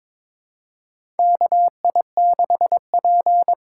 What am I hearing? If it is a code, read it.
Morse code, heard as a keyed tone, about 22 wpm, 700 Hz. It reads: KI6P
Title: USA Call Signs - 6 Characters - 6x Character Spacing - 22wpm